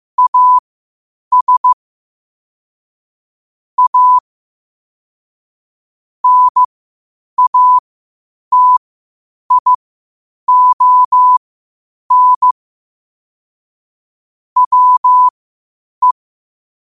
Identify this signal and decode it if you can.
Morse code, heard as a keyed tone, about 15 wpm, spaced as for 8 wpm, 1000 Hz. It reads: AS A NATION WE
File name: qotdfiles / 8wpm